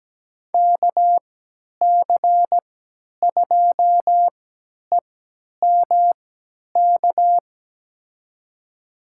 Morse code, heard as a keyed tone, about 17 wpm, spaced as for 10 wpm, 700 Hz. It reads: KC2EMK
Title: USA Call Signs - 6 Characters - 3x Character Spacing - 17wpm